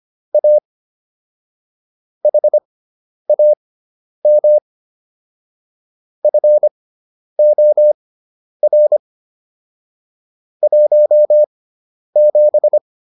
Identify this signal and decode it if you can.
Morse code, heard as a keyed tone, about 25 wpm, spaced as for 10 wpm, 600 Hz. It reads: A HAM FOR 17